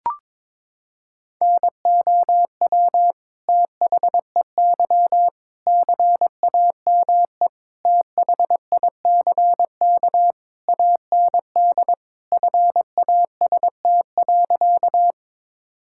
Morse code, heard as a keyed tone, about 22 wpm, 700 Hz. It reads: NOW THEY CAME THICK AND FAST.
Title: Sentences from Top 1600 Words 22wpm